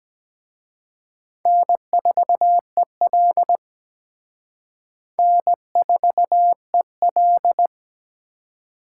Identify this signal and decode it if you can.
Morse code, heard as a keyed tone, about 20 wpm, 700 Hz. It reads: N4EL N4EL
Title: Speed Racing - USA Call Signs 17wpm